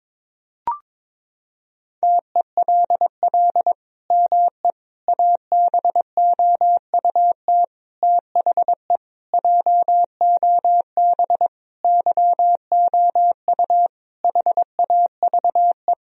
Morse code, heard as a keyed tone, about 22 wpm, 700 Hz. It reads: TELL ME ABOUT THE JOB YOU HAVE
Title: Sentences from Top 1000 Words - Review Part 3 - 22wpm